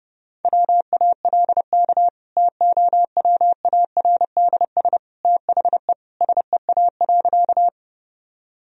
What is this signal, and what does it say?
Morse code, heard as a keyed tone, about 32 wpm, 700 Hz. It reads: WALK TOWARDS THE SEA.
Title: Sentences from Top 1000 Words - Review Part 2 - 30wpm